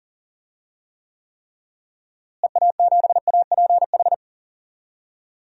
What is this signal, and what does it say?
Morse code, heard as a keyed tone, about 41 wpm, 700 Hz. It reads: EA7APH
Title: DXCC-100 Call Signs - 6 Characters - Repeated 3x - 40wpm